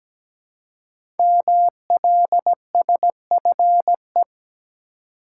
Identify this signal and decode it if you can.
Morse code, heard as a keyed tone, about 17 wpm, 700 Hz. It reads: MLSFE
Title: Code Groups - 5 Letters - 17wpm